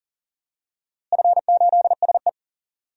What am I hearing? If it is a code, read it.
Morse code, heard as a keyed tone, about 40 wpm, 700 Hz. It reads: F8SE
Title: DXCC-100 Call Signs - 4 Characters - Repeated 2x - 40wpm